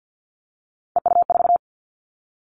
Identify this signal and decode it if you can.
Morse code, heard as a keyed tone, about 50 wpm, 700 Hz. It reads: EU4
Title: Callsign Prefixes for Top 50 DXCC - 50wpm